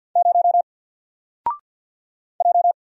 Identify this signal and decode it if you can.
Morse code, heard as a keyed tone, about 48 wpm, 700 Hz. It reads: 0 J
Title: Single Character (Letter:Number:Punctuation:Prosign) Alt 50wpm